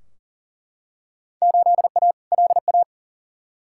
Morse code, heard as a keyed tone, about 37 wpm, 700 Hz. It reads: 8A LA
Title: Field Day Exchanges Encore - 40wpm